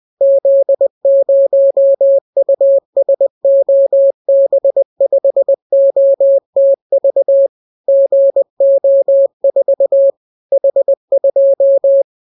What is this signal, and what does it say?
Morse code, heard as a keyed tone, about 20 wpm, 550 Hz. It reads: Z0USOB5OTV GO4 H2